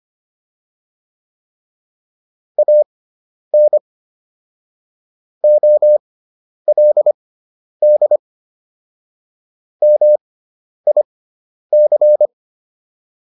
Morse code, heard as a keyed tone, about 25 wpm, spaced as for 10 wpm, 600 Hz. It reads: AN OLD MIC